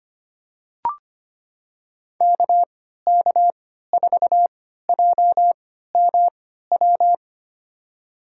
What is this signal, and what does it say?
Morse code, heard as a keyed tone, about 25 wpm, spaced as for 14 wpm, 700 Hz. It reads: KK4JMW
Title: USA Call Signs - 6 Characters - 3x Character Spacing - 25wpm